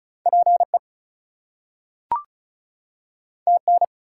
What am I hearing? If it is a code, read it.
Morse code, heard as a keyed tone, about 36 wpm, 700 Hz. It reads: PE TN
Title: US States and Canadian Provinces 35wpm